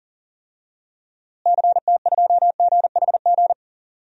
Morse code, heard as a keyed tone, about 40 wpm, 700 Hz. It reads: CT2GHG
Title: DXCC-100 Call Signs - 6 Characters - Repeated 3x - 40wpm